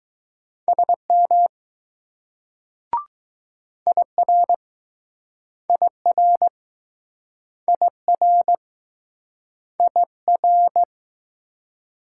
Morse code, heard as a keyed tone, about 21 wpm, 700 Hz. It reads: SM IR IR IR IR